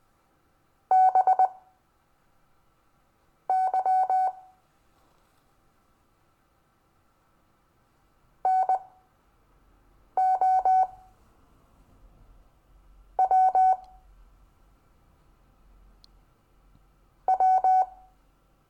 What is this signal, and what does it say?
Morse code, heard as a keyed tone, about 19 wpm, spaced as for 4 wpm, 750 Hz. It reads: BY NOW W
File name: cw / Lesson26 SUMMARY